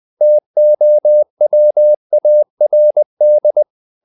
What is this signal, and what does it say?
Morse code, heard as a keyed tone, about 20 wpm, 600 Hz. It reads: TOWARD
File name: the old man and the sea / the old man and the sea part 1 20wpm